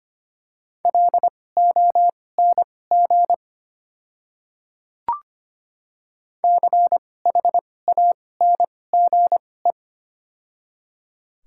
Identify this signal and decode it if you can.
Morse code, heard as a keyed tone, about 25 wpm, spaced as for 18 wpm, 700 Hz. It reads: LONG CHANGE
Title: 2x Character Spacing - Top 300 Words - 25wpm